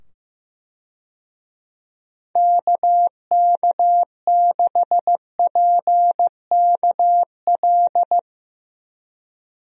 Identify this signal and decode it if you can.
Morse code, heard as a keyed tone, about 15 wpm, 700 Hz. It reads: KK6PKL
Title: USA Call Signs - 6 Characters - 15wpm